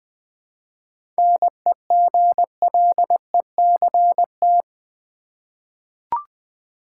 Morse code, heard as a keyed tone, about 20 wpm, 690 Hz. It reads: NEGLECT
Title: 7 Letter Words - First 3 Letters Given - 20wpm